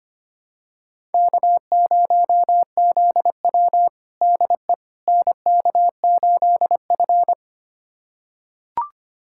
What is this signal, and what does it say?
Morse code, heard as a keyed tone, about 25 wpm, 700 Hz. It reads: K0ZW DE NK8F
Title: USA Callsign DE Callsign - 4 Characters - 25wpm